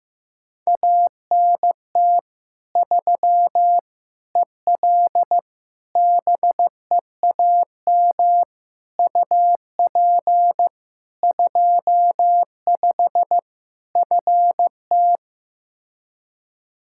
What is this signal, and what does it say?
Morse code, heard as a keyed tone, about 15 wpm, 700 Hz. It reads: ANT 3 EL BEAM UP 25 FT
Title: Practice QSO - Exchange 2 - Info - 15wpm